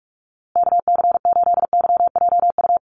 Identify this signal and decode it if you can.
Morse code, heard as a keyed tone, about 44 wpm, 700 Hz. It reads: KC8YJU
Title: USA Call Signs Repeated 3x 45wpm